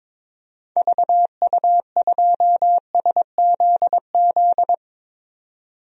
Morse code, heard as a keyed tone, about 22 wpm, 700 Hz. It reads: VU2SZZ
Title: DXCC-100 Call Signs - 6 Characters - 22wpm